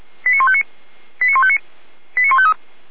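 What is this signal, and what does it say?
Five-tone selective call (ZVEI), heard as quick runs of five tones, tones of about 70 ms. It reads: 89159 89159 89153
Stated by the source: ZVEI five-tone call 89150 received by SDR.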